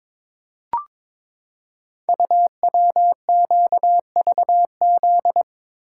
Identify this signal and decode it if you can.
Morse code, heard as a keyed tone, about 22 wpm, 700 Hz. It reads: UWQVZ
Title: Code Groups - 5 Letters - 22wpm